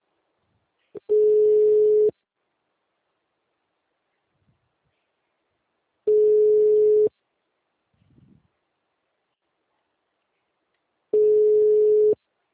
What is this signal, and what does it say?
Ringback tone (Europe, 425 Hz).
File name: out- -2023